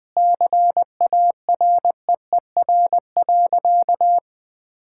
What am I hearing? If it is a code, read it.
Morse code, heard as a keyed tone, about 20 wpm, 700 Hz. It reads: CAREER.
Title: Sentences with 7-Word Phrases - 20wpm